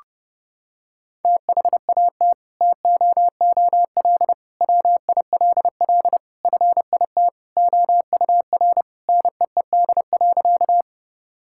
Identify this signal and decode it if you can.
Morse code, heard as a keyed tone, about 30 wpm, 700 Hz. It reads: THAT TOOL WILL FIT OUR NEED.
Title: Sentences from Top 1000 Words - Review Part 6 - 30wpm